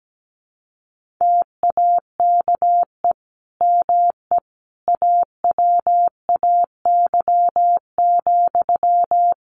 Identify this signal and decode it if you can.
Morse code, heard as a keyed tone, about 17 wpm, 700 Hz. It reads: TAKE ME AWAY,
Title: Sentences from Top 1400 Words 17wpm